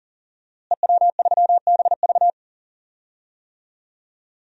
Morse code, heard as a keyed tone, about 40 wpm, 700 Hz. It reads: EW3BV
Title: DXCC-100 Call Signs - 5 Characters - Repeated 2x - 40wpm